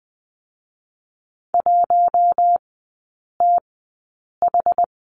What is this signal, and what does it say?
Morse code, heard as a keyed tone, about 20 wpm, spaced as for 8 wpm, 700 Hz. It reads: 1TH
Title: 093 Lesson 16e - Letter c - Three Characters 20wpm